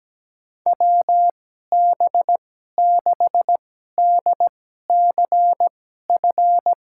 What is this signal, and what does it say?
Morse code, heard as a keyed tone, about 17 wpm, spaced as for 12 wpm, 700 Hz. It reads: WB6DCF
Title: USA Call Signs - 6 Characters - 2x Character Spacing - 17wpm